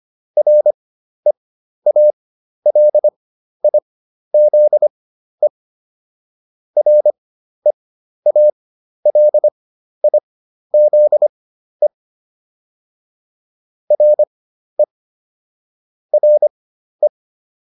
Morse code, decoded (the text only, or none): REALIZE REALIZE RE RE